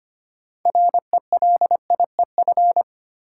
Morse code, heard as a keyed tone, about 25 wpm, 700 Hz. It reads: RELIEF
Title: Six Letter Words - 25wpm